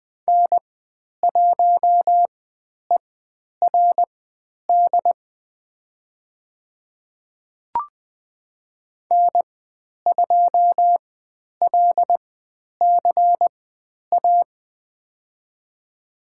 Morse code, heard as a keyed tone, about 20 wpm, spaced as for 10 wpm, 700 Hz. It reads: N1ERD N2LCA
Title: 109 Lesson 18g - Letter m - Call Signs 20wpm